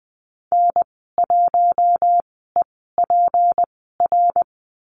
Morse code, heard as a keyed tone, about 19 wpm, spaced as for 14 wpm, 700 Hz. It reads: N1EPR
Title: 245 Lesson 39g - Letter z - Call Signs 20wpm